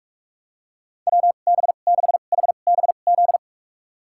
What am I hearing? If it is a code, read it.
Morse code, heard as a keyed tone, about 46 wpm, spaced as for 32 wpm, 700 Hz. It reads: WB6HBZ
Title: USA Call Signs - 6 Characters - 2x Character Spacing - 45wpm